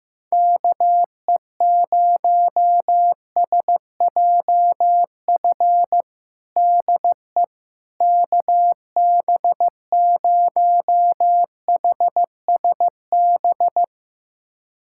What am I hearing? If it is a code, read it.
Morse code, heard as a keyed tone, about 15 wpm, 700 Hz. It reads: KE0SJF DE KB0HSB
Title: USA Callsign DE Callsign - 6 Characters - 15wpm